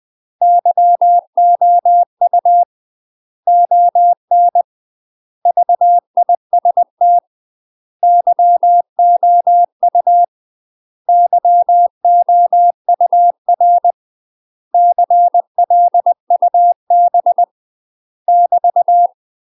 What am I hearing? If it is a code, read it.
Morse code, heard as a keyed tone, about 20 wpm, 700 Hz. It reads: YOU ON VIST YOU YOUR CLUB <BT>
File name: UK MASTER QSOs TRAINING / 20WPM / UK MASTER QSOs 0053